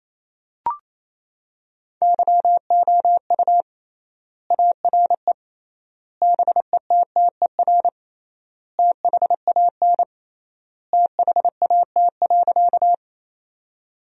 Morse code, heard as a keyed tone, about 28 wpm, 700 Hz. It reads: YOU ARE BETTER THAN THAT.
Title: 3x Word-Spacing - Sentences from Top 400 Words - 28wpm